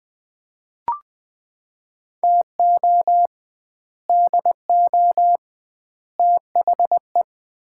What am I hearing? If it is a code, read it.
Morse code, heard as a keyed tone, about 21 wpm, 700 Hz. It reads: TO DO THE
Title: 2x Word-Spacing - Sets of 5 Words - 20wpm